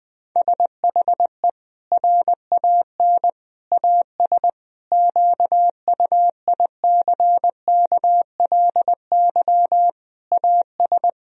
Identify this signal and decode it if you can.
Morse code, heard as a keyed tone, about 20 wpm, 700 Hz. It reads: SHE RAN AS QUICKLY AS
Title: Sentences from Top 1000 Words - Review Part 4 - 20wpm